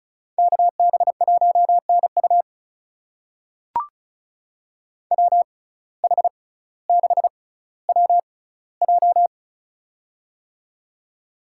Morse code, heard as a keyed tone, about 36 wpm, spaced as for 18 wpm, 700 Hz. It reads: KB1NU WH6WJ
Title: USA Call Signs - 5 Characters - 6x Character Spacing - 35wpm